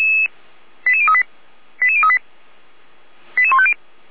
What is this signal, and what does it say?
Five-tone selective call (ZVEI), heard as quick runs of five tones, tones of about 70 ms. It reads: 88038 88038 88150
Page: SDR/ZVEI